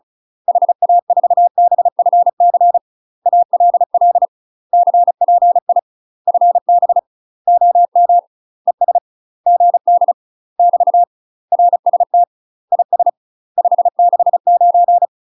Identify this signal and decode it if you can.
Morse code, heard as a keyed tone, about 36 wpm, 700 Hz. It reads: HA4BFC ALL CPI FB OM ES GD <BT> RST IS 569
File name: UK MASTER QSOs TRAINING / 35WPM / UK MASTER QSOs 0010